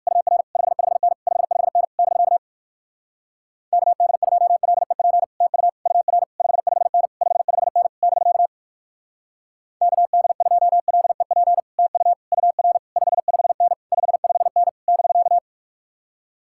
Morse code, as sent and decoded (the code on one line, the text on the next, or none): ..- .-. / ..... ..... -. / ..... ..... -. / -...-.- / -.- -.. ..--- .-.. .--. / - ..- / ..- .-. / ..... ..... -. / ..... ..... -. / -...-.- / -.- -.. ..--- .-.. .--. / - ..- / ..- .-. / ..... ..... -. / ..... ..... -. / -...-.-
UR 55N 55N <BK> KD2LEP TU UR 55N 55N <BK> KD2LEP TU UR 55N 55N <BK>